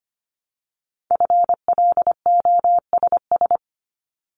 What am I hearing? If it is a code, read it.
Morse code, heard as a keyed tone, about 25 wpm, 700 Hz. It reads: FLOSS
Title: Multi-letter Phonemes Repeated 3x 25wpm